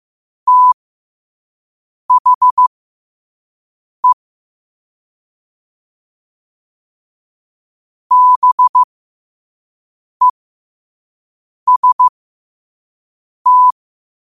Morse code, heard as a keyed tone, about 15 wpm, spaced as for 5 wpm, 1000 Hz. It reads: THE BEST
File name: qotdfiles / 5wpm